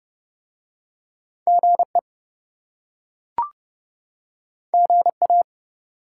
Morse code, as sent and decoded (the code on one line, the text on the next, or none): --. / --. .-
GE GA